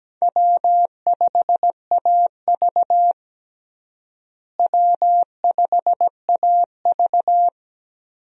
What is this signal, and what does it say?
Morse code, heard as a keyed tone, about 17 wpm, 700 Hz. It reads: W5AV W5AV
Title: USA Call Signs - 4 Characters - Repeated 3x - 17wpm